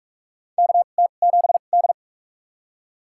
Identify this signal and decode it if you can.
Morse code, heard as a keyed tone, about 44 wpm, spaced as for 32 wpm, 700 Hz. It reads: KT7D
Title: USA Call Signs - 4 Characters - 2x Character Spacing - 45wpm